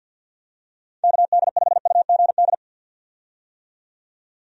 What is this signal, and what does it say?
Morse code, heard as a keyed tone, about 51 wpm, 700 Hz. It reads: KD5UGD
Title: USA Call Signs - 6 Characters - 50wpm